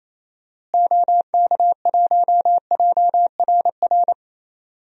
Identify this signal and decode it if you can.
Morse code, heard as a keyed tone, about 28 wpm, 700 Hz. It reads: OK1JRR
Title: DXCC-100 Call Signs - 6 Characters - Repeated 3x - 28wpm